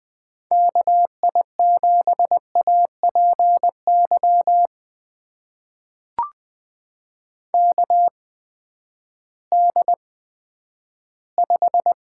Morse code, heard as a keyed tone, about 20 wpm, spaced as for 11 wpm, 700 Hz. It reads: KI7APY KD5
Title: USA Call Signs - 6 Characters - 8x Character Spacing - 20wpm